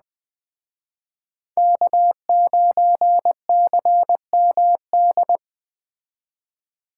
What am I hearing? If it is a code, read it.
Morse code, heard as a keyed tone, about 21 wpm, 700 Hz. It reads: K9CMD